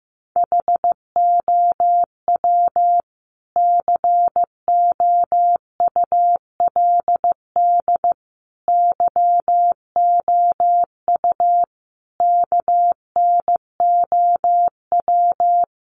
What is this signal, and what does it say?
Morse code, heard as a keyed tone, about 15 wpm, 700 Hz. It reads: HOW COULD YOU KNOW
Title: Sentences from Top 1000 Words 15wpm